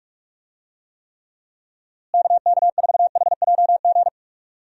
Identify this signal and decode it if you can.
Morse code, heard as a keyed tone, about 44 wpm, 700 Hz. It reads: KK4HJG
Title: USA Call Signs - 6 Characters - Repeated 3x - 45wpm